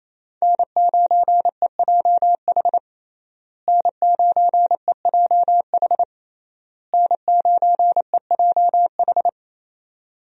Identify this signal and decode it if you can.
Morse code, heard as a keyed tone, about 28 wpm, 700 Hz. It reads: N9EJH N9EJH N9EJH
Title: USA Call Signs - 5 Characters - Repeated 3x - 28wpm